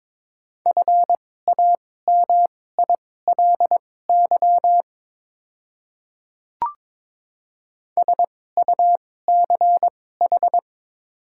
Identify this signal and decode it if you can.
Morse code, heard as a keyed tone, about 22 wpm, spaced as for 16 wpm, 690 Hz. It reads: FAMILY SUCH
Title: 2x Character Spacing - Top 500 Words - 22wpm